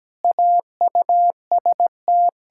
Morse code, heard as a keyed tone, about 17 wpm, 700 Hz. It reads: AUST
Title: Practice QSO - Exchange 1 - Beginning - 17wpm